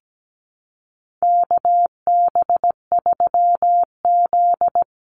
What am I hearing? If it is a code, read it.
Morse code, heard as a keyed tone, about 17 wpm, 700 Hz. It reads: KB3Z
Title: Call Signs Repeated 2x - 17wpm